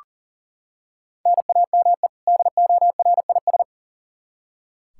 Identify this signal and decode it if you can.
Morse code, heard as a keyed tone, about 40 wpm, 700 Hz. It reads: NAME DORIS
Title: Practice QSO - Exchange 1 - Beginning - 40wpm